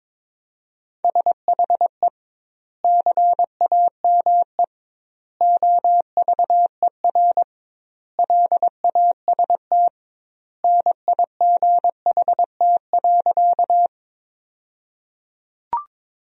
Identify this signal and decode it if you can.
Morse code, heard as a keyed tone, about 22 wpm, 700 Hz. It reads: SHE CAME OVER LAST NIGHT.
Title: 2x Word-Spacing - Sentences from Top 300 Words - 22wpm